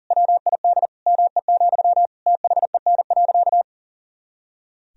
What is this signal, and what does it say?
Morse code, heard as a keyed tone, about 40 wpm, 700 Hz. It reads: WID ME, THEN.
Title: Sentences from Top 200 Words and Abbreviations 40wpm